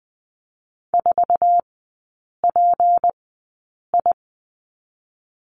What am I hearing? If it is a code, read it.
Morse code, heard as a keyed tone, about 20 wpm, spaced as for 8 wpm, 700 Hz. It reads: 4PI
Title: 172 Lesson 28e - Number 9 - Three Characters 20wpm